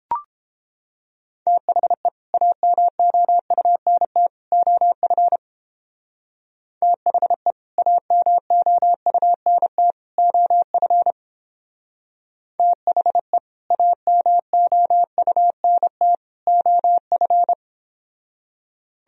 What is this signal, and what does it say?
Morse code, heard as a keyed tone, about 31 wpm, 700 Hz. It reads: THE AMOUNT OF THE AMOUNT OF THE AMOUNT OF